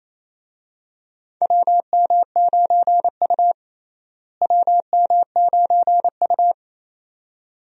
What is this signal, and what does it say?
Morse code, heard as a keyed tone, about 28 wpm, 700 Hz. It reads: WM9U WM9U
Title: USA Call Signs - 4 Characters - Repeated 2x - 28wpm